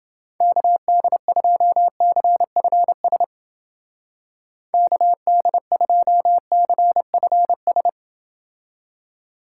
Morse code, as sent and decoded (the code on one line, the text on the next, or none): -.- -.. ..--- -.-. ..-. ... / -.- -.. ..--- -.-. ..-. ...
KD2CFS KD2CFS